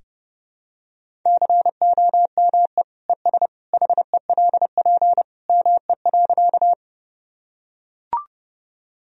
Morse code, heard as a keyed tone, about 32 wpm, 700 Hz. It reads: COME ES HELP ME.
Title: Sentences from Top 200 Words and Abbreviations 30wpm